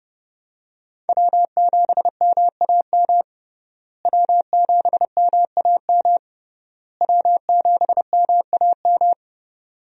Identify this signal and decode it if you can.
Morse code, heard as a keyed tone, about 30 wpm, 700 Hz. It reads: W7MAM W7MAM W7MAM
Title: USA Call Signs - 5 Characters - Repeated 3x - 30wpm